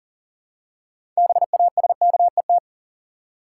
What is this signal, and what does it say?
Morse code, heard as a keyed tone, about 40 wpm, 700 Hz. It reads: BASKET
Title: Six Letter Words - 40wpm